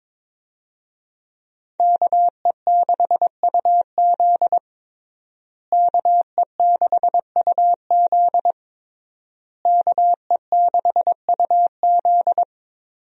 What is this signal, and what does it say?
Morse code, heard as a keyed tone, about 22 wpm, 700 Hz. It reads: KE6UZ KE6UZ KE6UZ
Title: USA Call Signs - 5 Characters - Repeated 3x - 22wpm